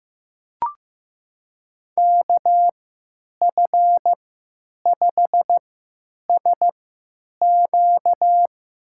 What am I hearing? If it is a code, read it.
Morse code, heard as a keyed tone, about 15 wpm, spaced as for 9 wpm, 700 Hz. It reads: KF5SQ
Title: USA Call Signs - 3x Character Spacing - 15wpm